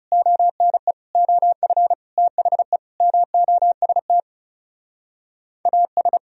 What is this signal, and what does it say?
Morse code, heard as a keyed tone, about 38 wpm, 700 Hz. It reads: ONE OF THE MOST AS
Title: Speed-Racing - Sets of 5 Words 30wpm